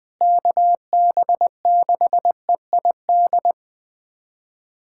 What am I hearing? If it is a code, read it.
Morse code, heard as a keyed tone, about 20 wpm, 700 Hz. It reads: KB6EID
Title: USA Callsign DE Callsign - 6 Characters - 20wpm